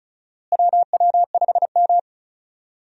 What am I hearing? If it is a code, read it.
Morse code, heard as a keyed tone, about 34 wpm, 700 Hz. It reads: WW5M
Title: USA Call Signs - 4 Characters - Repeated 2x - 35wpm